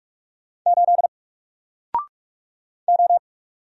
Morse code, heard as a keyed tone, about 44 wpm, 700 Hz. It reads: 8 O
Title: Single Letter-Number 45wpm